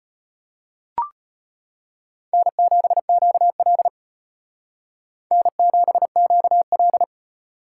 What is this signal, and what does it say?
Morse code, heard as a keyed tone, about 33 wpm, 700 Hz. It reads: N7QL N7QL